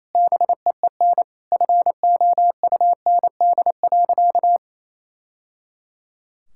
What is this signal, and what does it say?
Morse code, heard as a keyed tone, about 28 wpm, 700 Hz. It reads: BEEN FOUND.
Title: Sentences from Top 1000 Words - Review Part 5 - 28wpm